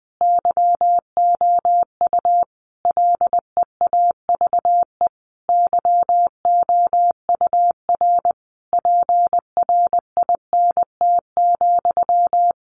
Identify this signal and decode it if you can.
Morse code, heard as a keyed tone, about 20 wpm, 700 Hz. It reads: YOU LEAVE YOUR PRINT,
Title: A Sound Of Thunder 20wpm